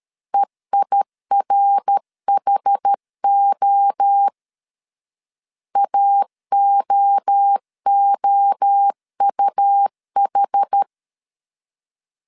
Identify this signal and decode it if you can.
Morse code, heard as a keyed tone, about 13 wpm, 800 Hz. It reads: EIRHO AOOUH
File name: CW / DET / 1 3 1500C S B 12wpm 2828